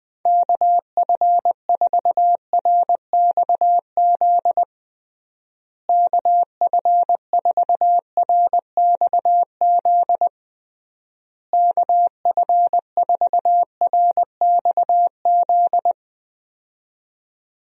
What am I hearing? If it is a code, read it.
Morse code, heard as a keyed tone, about 19 wpm, 700 Hz. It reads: KF4RXZ KF4RXZ KF4RXZ
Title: USA Call Signs - 6 Characters - Repeated 3x - 20wpm